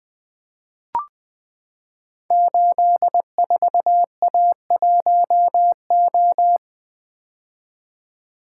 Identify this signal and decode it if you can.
Morse code, heard as a keyed tone, about 20 wpm, 700 Hz. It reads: 84A1O
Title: Code Groups - 5 Letter:Numbers - 20wpm